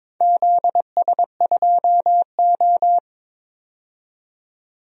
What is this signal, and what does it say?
Morse code, heard as a keyed tone, about 22 wpm, 700 Hz. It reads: ZS2O
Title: DXCC-100 Call Signs - 4 Characters - 22wpm